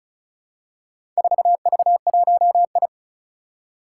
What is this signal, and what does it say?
Morse code, heard as a keyed tone, about 36 wpm, 700 Hz. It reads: 4V1I
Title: Four Challenging Letter-Number Combinations 35wpm